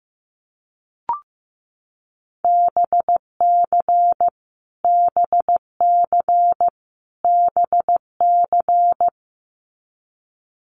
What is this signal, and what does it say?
Morse code, heard as a keyed tone, about 15 wpm, 700 Hz. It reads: BC BC BC
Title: Canadian Provinces and Territories - Repeated 3x - 15wpm